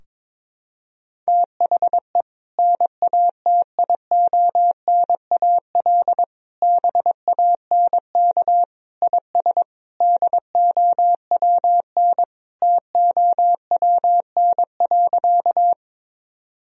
Morse code, heard as a keyed tone, about 22 wpm, 700 Hz. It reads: THE NATIONAL BANK IS DOWN TOWN.
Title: Sentences from Top 1000 Words - Review Part 4 - 22wpm